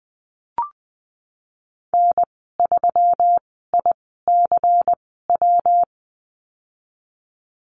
Morse code, heard as a keyed tone, about 20 wpm, spaced as for 14 wpm, 700 Hz. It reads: N3ICW
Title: 167 Lesson 27g - Number 7 - Call Signs 20wpm